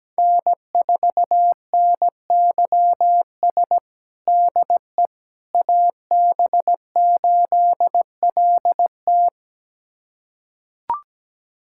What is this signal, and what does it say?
Morse code, heard as a keyed tone, about 17 wpm, 700 Hz. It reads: N4NYS DE AB8LT
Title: USA Callsign DE Callsign - 5 Characters - 17wpm